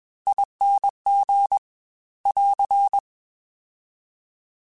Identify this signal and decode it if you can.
Morse code, heard as a keyed tone, about 21 wpm, 800 Hz. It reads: ING <AR>